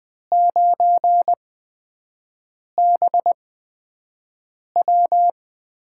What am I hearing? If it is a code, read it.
Morse code, heard as a keyed tone, about 20 wpm, spaced as for 5 wpm, 700 Hz. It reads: 9BW
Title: USA Call Signs - 6 Characters - 8x Character Spacing - 20wpm